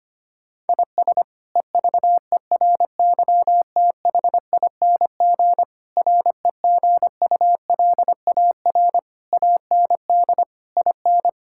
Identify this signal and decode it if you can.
Morse code, heard as a keyed tone, about 25 wpm, 700 Hz. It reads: IS EVERYTHING REGULAR AND IN
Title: Sentences from Top 1000 Words - Review Part 6 - 25wpm